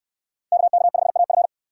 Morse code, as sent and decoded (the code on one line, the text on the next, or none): -.- -.- ..... .- ..-.
KK5AF